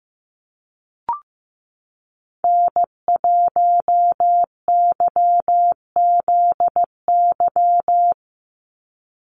Morse code, heard as a keyed tone, about 15 wpm, 700 Hz. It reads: N1YZY
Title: USA Call Signs 15wpm